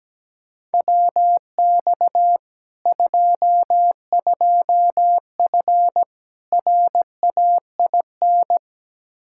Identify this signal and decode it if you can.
Morse code, heard as a keyed tone, about 17 wpm, 700 Hz. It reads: WX 22F RAIN
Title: Practice QSO - Exchange 2 - Info - 17wpm